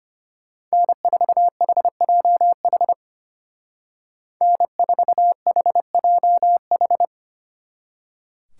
Morse code, heard as a keyed tone, about 29 wpm, 700 Hz. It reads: N4HJH N4HJH
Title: Speed Racing - USA 1x3 Call Signs 25wpm